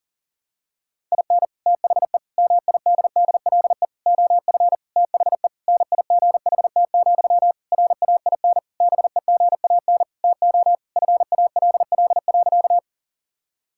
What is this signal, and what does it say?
Morse code, heard as a keyed tone, about 40 wpm, 690 Hz. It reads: IN THE MIDDLE OF THE NIGHT, RAIN BEGAN TO FALL.
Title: Sentences from Top 1000 Words - Review Part 6 - 40wpm